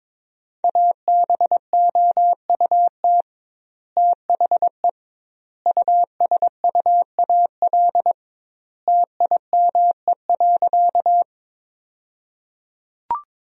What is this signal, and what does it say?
Morse code, heard as a keyed tone, about 22 wpm, 700 Hz. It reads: ABOUT THE USUAL TIME.
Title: 2x Word-Spacing - Sentences from Top 400 Words - 22wpm